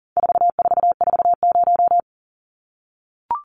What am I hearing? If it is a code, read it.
Morse code, heard as a keyed tone, about 38 wpm, 700 Hz. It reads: 4440
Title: Numbers - 4-Digit 40wpm